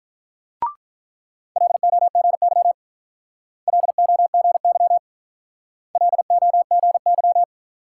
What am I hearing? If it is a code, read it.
Morse code, heard as a keyed tone, about 45 wpm, 700 Hz. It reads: LOGY LOGY LOGY